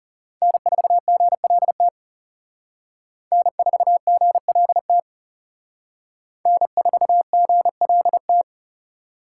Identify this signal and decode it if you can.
Morse code, heard as a keyed tone, about 34 wpm, 700 Hz. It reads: N4GLT N4GLT N4GLT
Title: Speed Racing - USA 1x3 Call Signs 30wpm